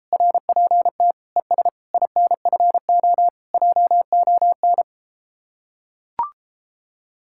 Morse code, heard as a keyed tone, about 33 wpm, 700 Hz. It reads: RPT ES INFO JON